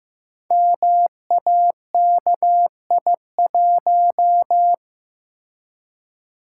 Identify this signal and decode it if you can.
Morse code, heard as a keyed tone, about 15 wpm, 700 Hz. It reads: MAKI1
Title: Code Groups - 5 Letter:Numbers - 15wpm